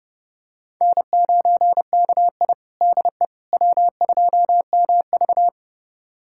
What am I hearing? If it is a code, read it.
Morse code, heard as a keyed tone, about 30 wpm, 700 Hz. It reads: N9KI DE W2MV
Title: USA Callsign DE Callsign - 4 Characters - 30wpm